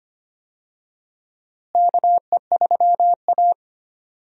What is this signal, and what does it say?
Morse code, heard as a keyed tone, about 25 wpm, 700 Hz. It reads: KE3A
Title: USA Call Signs - 4 Characters - Repeated 2x - 25wpm